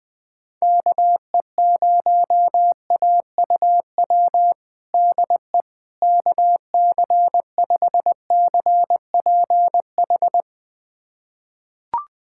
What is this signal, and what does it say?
Morse code, heard as a keyed tone, about 20 wpm, 700 Hz. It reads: KE0AUW DE KC5CPH
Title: USA Callsign DE Callsign - 6 Characters - 20wpm